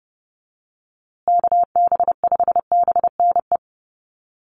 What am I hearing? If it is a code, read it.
Morse code, heard as a keyed tone, about 30 wpm, 700 Hz. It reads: KB5BNE
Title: USA Call Signs Repeated 3x 30wpm